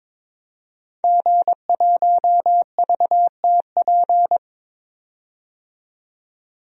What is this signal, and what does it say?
Morse code, heard as a keyed tone, about 22 wpm, 700 Hz. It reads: G1VTP
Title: DXCC-100 Call Signs - 5 Characters - 22wpm